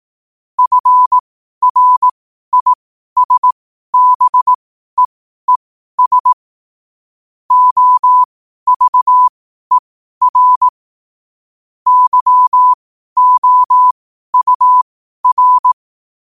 Morse code, heard as a keyed tone, about 18 wpm, spaced as for 12 wpm, 1000 Hz. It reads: FRISBEES OVER YOUR
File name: qotdfiles / 12wpm